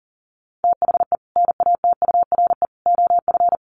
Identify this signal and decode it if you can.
Morse code, heard as a keyed tone, about 40 wpm, 700 Hz. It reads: THE NATURE OF
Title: Sets of 3 Words 40wpm